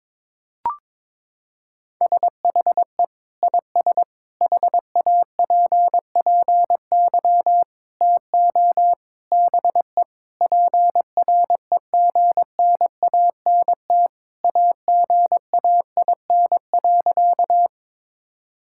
Morse code, heard as a keyed tone, about 22 wpm, 700 Hz. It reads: SHE IS HAPPY TO BE PREGNANT AGAIN.
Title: Sentences from Top 1000 Words - Review Part 6 - 22wpm